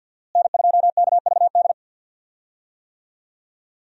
Morse code, heard as a keyed tone, about 48 wpm, 700 Hz. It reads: N2KVD
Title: USA Call Signs - 5 Characters - Repeated 2x - 50wpm